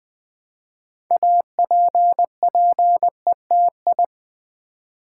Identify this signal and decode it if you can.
Morse code, heard as a keyed tone, about 20 wpm, 700 Hz. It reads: APPETI
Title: Build A Word - 3 Letters At A Time - 8-Letter Words - 20wpm